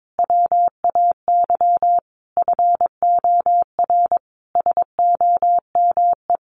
Morse code, heard as a keyed tone, about 22 wpm, 700 Hz. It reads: WAY FOR SOME
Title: Sentences from Top 700 Words 22wpm